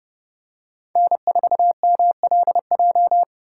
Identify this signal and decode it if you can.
Morse code, heard as a keyed tone, about 29 wpm, 700 Hz. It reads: N4MLJ